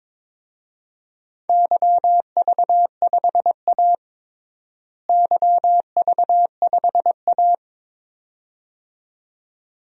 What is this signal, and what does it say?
Morse code, heard as a keyed tone, about 22 wpm, 700 Hz. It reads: YV5A YV5A
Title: DXCC-100 Call Signs - 4 Characters - Repeated 2x - 22wpm